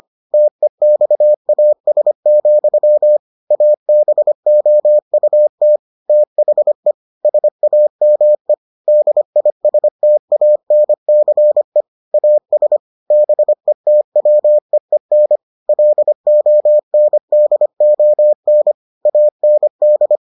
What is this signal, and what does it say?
Morse code, heard as a keyed tone, about 25 wpm, 600 Hz. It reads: TEXAS, ABOUT THE SAME DISTANCE AS BETWEEN LONDON AND